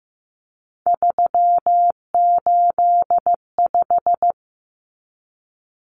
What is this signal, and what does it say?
Morse code, heard as a keyed tone, about 15 wpm, 700 Hz. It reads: 385
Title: Common Cut Numbers - 3-Digit 15wpm